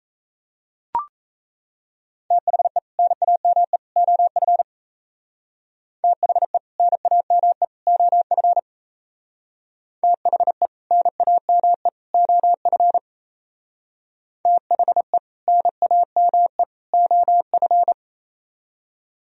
Morse code, heard as a keyed tone, about 35 wpm, 700 Hz. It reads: THE NAME OF THE NAME OF THE NAME OF THE NAME OF
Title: Speed-Racing - Sets of 3 Words 28wpm